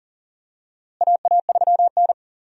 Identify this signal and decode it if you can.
Morse code, heard as a keyed tone, about 40 wpm, 700 Hz. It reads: AA3N
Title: USA Call Signs - 4 Characters - Repeated 2x - 40wpm